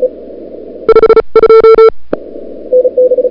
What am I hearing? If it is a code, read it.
Morse code, heard as a keyed tone, about 34 wpm, spaced as for 29 wpm, 1340 Hz, with noise. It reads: 52
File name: CWSprint / February 2007 CW Sprint / 01-2007 CW Sprint Hour1